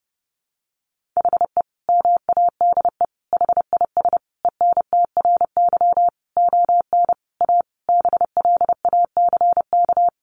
Morse code, heard as a keyed tone, about 30 wpm, 700 Hz. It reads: HE MADE HIS ENTRY ON A BLACK
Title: Sentences from Top 1500 Words 30wpm